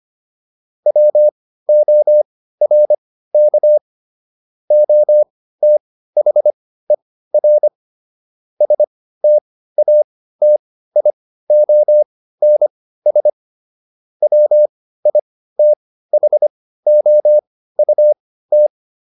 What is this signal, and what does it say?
Morse code, heard as a keyed tone, about 25 wpm, spaced as for 15 wpm, 600 Hz. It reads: WORK OTHER STATIONS WITHOUT